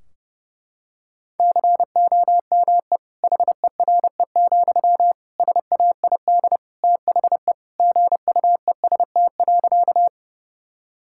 Morse code, heard as a keyed tone, about 29 wpm, 700 Hz. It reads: COME HERE, SAID THE GUEST.
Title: Sentences from Top 1600 Words 30wpm